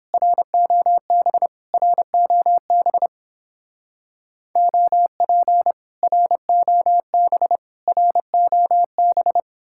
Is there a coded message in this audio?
Yes: Morse code from 0.1 to 9.4 s, heard as a keyed tone, about 28 wpm, 700 Hz. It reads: ROB ROB OP ROB ROB